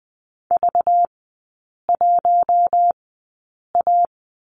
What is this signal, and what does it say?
Morse code, heard as a keyed tone, about 20 wpm, spaced as for 8 wpm, 700 Hz. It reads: V1A
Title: 218 Lesson 35e - Number 0 - Three Characters 20wpm